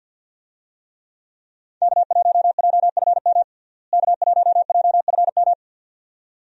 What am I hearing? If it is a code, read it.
Morse code, heard as a keyed tone, about 51 wpm, 700 Hz. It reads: K1JFM K1JFM
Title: USA Call Signs - 5 Characters - Repeated 2x - 50wpm